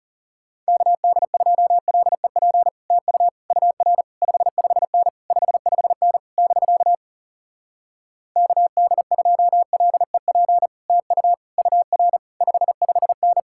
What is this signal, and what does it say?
Morse code, heard as a keyed tone, about 40 wpm, 700 Hz. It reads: KD2LEP TU UR 55N 55N <BK> KD2LEP TU UR 55N